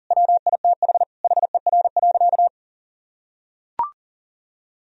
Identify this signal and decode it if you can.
Morse code, heard as a keyed tone, about 40 wpm, 700 Hz. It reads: WITH HER.
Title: Sentences from Top 1000 Words - Review Part 4 - 40wpm